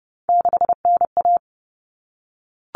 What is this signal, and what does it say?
Morse code, heard as a keyed tone, about 30 wpm, 700 Hz. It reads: 6NA
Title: Common Cut Numbers - 3-Digit 30wpm